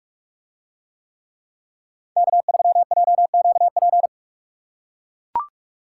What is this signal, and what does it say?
Morse code, heard as a keyed tone, about 43 wpm, 700 Hz. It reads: K3JQP